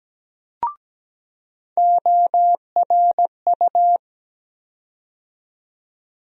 Morse code, heard as a keyed tone, about 17 wpm, 700 Hz. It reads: ORU
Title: USA Call Sign Postfixes - 3 Letters 17wpm